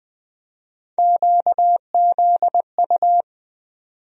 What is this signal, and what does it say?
Morse code, heard as a keyed tone, about 20 wpm, 700 Hz. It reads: QZU
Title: USA Call Sign Postfixes - 3 Letters 20wpm